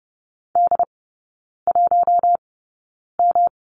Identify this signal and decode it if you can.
Morse code, heard as a keyed tone, about 30 wpm, spaced as for 9 wpm, 700 Hz. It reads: D1M
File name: Lesson 18e - Letter m - Three Characters 30wpm